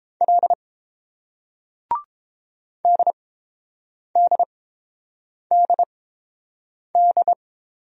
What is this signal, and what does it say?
Morse code, heard as a keyed tone, about 29 wpm, spaced as for 7 wpm, 700 Hz. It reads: L DDDD